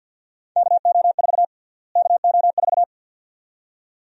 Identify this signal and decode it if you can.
Morse code, heard as a keyed tone, about 48 wpm, 700 Hz. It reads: KO4 KO4
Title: USA Call Sign Prefixes - Repeated 2x - 50wpm